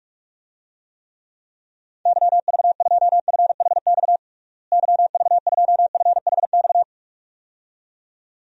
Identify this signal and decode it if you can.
Morse code, heard as a keyed tone, about 46 wpm, 700 Hz. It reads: YV2FHX YV2FHX
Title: DXCC-100 Call Signs - 6 Characters - Repeated 2x - 45wpm